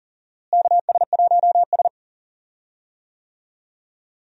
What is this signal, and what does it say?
Morse code, heard as a keyed tone, about 40 wpm, 700 Hz. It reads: KS1S
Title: USA Call Signs - 4 Characters - Repeated 2x - 40wpm